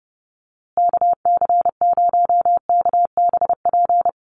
Morse code, heard as a keyed tone, about 30 wpm, 700 Hz. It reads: KC0KBP
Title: USA Call Signs Repeated 3x 30wpm